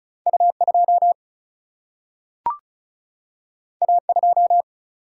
Morse code, heard as a keyed tone, about 36 wpm, 700 Hz. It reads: U 2 A 2
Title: Single-Letter Single-Number 35wpm